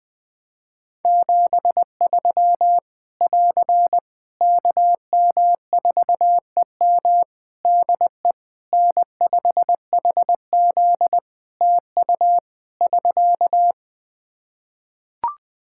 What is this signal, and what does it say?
Morse code, heard as a keyed tone, about 20 wpm, 700 Hz. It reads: 73 <AR> KM4EM DE N5HZ TU <SK>
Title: Practice QSO - Exchange 4 - Ending - 20wpm